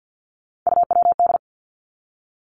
Morse code, heard as a keyed tone, about 50 wpm, 700 Hz. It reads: UWD
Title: Three Challenging Letter-Number Combinations 50wpm